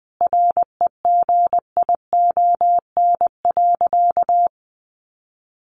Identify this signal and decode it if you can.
Morse code, heard as a keyed tone, about 19 wpm, 700 Hz. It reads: REGION.
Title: Sentences from Top 1500 Words 20wpm